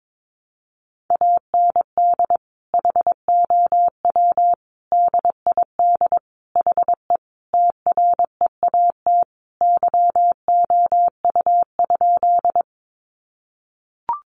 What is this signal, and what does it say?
Morse code, heard as a keyed tone, about 22 wpm, 700 Hz. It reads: AND HOW DID HE TREAT YOU?
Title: Sentences from Top 1300 Words 22wpm